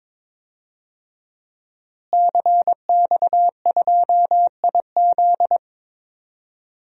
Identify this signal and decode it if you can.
Morse code, heard as a keyed tone, about 21 wpm, 700 Hz. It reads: CX2IZ
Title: DXCC-100 Call Signs - 5 Characters - Repeated 2x - 22wpm